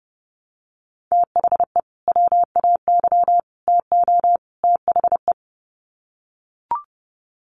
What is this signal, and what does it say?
Morse code, heard as a keyed tone, about 30 wpm, 700 Hz. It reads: THE WAY TO THE
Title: Sets of 4 Words 30wpm